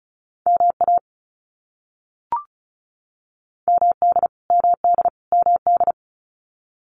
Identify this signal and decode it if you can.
Morse code, heard as a keyed tone, about 34 wpm, 700 Hz. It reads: MA MD MD MD
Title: US States Repeated 3x 35wpm